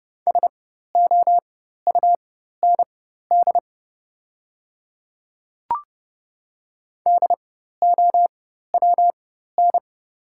Morse code, heard as a keyed tone, about 30 wpm, spaced as for 14 wpm, 700 Hz. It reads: SOUND DOWN
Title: 4x Character Spacing - Top 100 Words - 30wpm